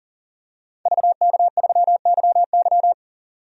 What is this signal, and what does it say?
Morse code, heard as a keyed tone, about 40 wpm, 700 Hz. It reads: VK3YY
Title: DXCC-100 Call Signs - 5 Characters - 40wpm